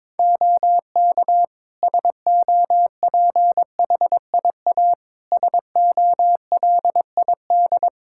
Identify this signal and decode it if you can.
Morse code, heard as a keyed tone, about 22 wpm, 700 Hz. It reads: OK SOPHIA SOLID
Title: Practice QSO - Exchange 3 - More Info - 22wpm